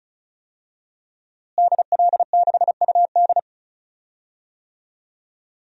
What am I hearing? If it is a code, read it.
Morse code, heard as a keyed tone, about 36 wpm, 700 Hz. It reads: DL6UD
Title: DXCC-100 Call Signs - 5 Characters - Repeated 3x - 35wpm